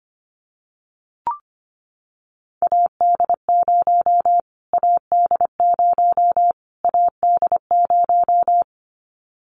Morse code, heard as a keyed tone, about 25 wpm, 700 Hz. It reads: AD0 AD0 AD0
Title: Callsign Prefixes for Top 25 DXCC - Repeated 3x - 25wpm